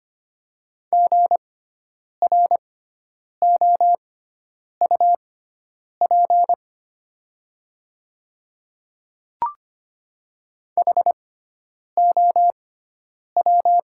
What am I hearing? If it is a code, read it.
Morse code, heard as a keyed tone, about 25 wpm, spaced as for 9 wpm, 700 Hz. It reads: GROUP HOW
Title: 6x Character Spacing - Top 300 Words - 25wpm